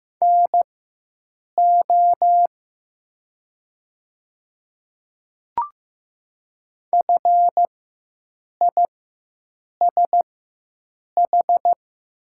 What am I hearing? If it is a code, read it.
Morse code, heard as a keyed tone, about 15 wpm, spaced as for 7 wpm, 700 Hz. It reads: NO FISH
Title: 4x Character Spacing - Top 300 Words - 15wpm